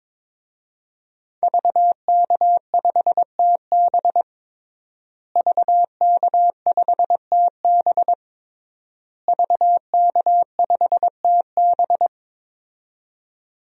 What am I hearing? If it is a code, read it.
Morse code, heard as a keyed tone, about 22 wpm, 700 Hz. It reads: VK5TB VK5TB VK5TB
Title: DXCC-100 Call Signs - 5 Characters - Repeated 3x - 22wpm